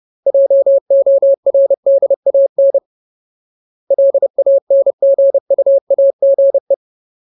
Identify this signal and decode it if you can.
Morse code, heard as a keyed tone, about 30 wpm, 550 Hz. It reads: JORDAN LANGUAGE
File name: Ultimate Word List / 30WPM Ultimate